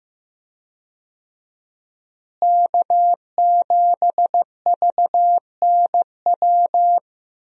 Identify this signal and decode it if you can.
Morse code, heard as a keyed tone, about 15 wpm, 700 Hz. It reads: K7VNW
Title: USA Call Signs - 5 Characters - Repeated 2x - 15wpm